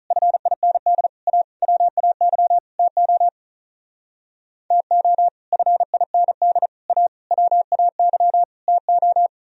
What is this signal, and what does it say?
Morse code, heard as a keyed tone, about 43 wpm, 700 Hz. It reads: FIND A WAY TO TO FIND A WAY TO
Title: Speed-Racing - Sets of 5 Words 35wpm